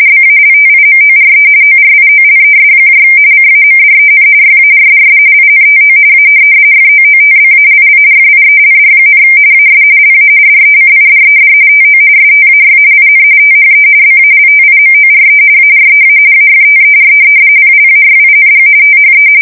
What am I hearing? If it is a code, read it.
RTTY, about 45.45 baud. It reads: BALTIMORE ORIOLES CAMDEN YARDS INTO A FOOTBALL FIELD, AS LA DECIDED TO STAY IN THE BALTIMORE AREA AFTER PLAYING THE
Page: 170 Shift 2125/2295